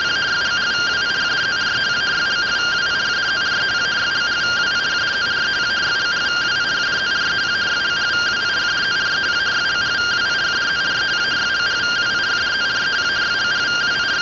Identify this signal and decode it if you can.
Two-tone teleprinter signal, about 170 Hz shift.